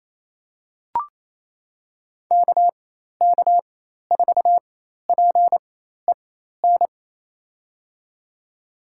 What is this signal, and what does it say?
Morse code, heard as a keyed tone, about 28 wpm, spaced as for 13 wpm, 700 Hz. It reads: KK4PEN
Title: USA Call Signs - 6 Characters - 4x Character Spacing - 28wpm